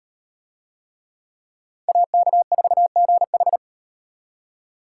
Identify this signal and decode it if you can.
Morse code, heard as a keyed tone, about 34 wpm, 700 Hz. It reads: AK4GH